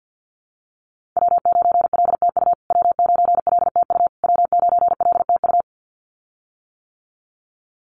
Morse code, heard as a keyed tone, about 49 wpm, 700 Hz. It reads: W9LTU W9LTU W9LTU
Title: USA Call Signs Repeated 3x 50wpm